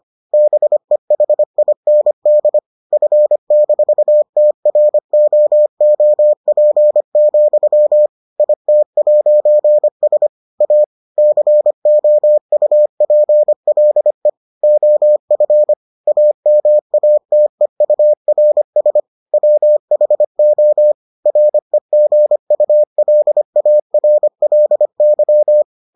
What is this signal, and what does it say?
Morse code, heard as a keyed tone, about 25 wpm, 600 Hz. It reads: BEHIND F-TROOP, IT'S A COUPLE OF AMATEURS WHO REGULARLY